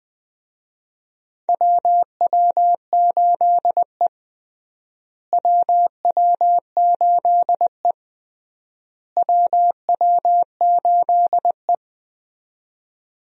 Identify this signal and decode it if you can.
Morse code, heard as a keyed tone, about 20 wpm, 700 Hz. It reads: WW8E WW8E WW8E
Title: USA Call Signs - 4 Characters - Repeated 3x - 20wpm